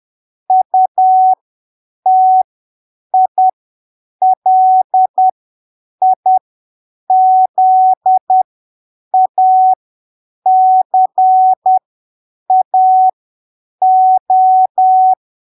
Morse code, heard as a keyed tone, about 10 wpm, spaced as for 7 wpm, 750 Hz. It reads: UTILIZACAO
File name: arquivos / texto / texto 06 10ppm